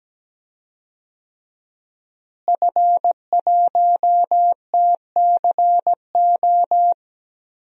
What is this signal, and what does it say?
Morse code, heard as a keyed tone, about 17 wpm, 700 Hz. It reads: F1TCO
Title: DXCC-100 Call Signs - 5 Characters - Repeated 3x - 17wpm